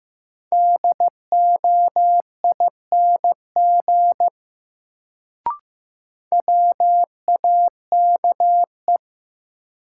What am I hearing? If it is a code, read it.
Morse code, heard as a keyed tone, about 15 wpm, 700 Hz. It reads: DOING WAKE
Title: Sets of 4 Words - New Word Drill - 15wpm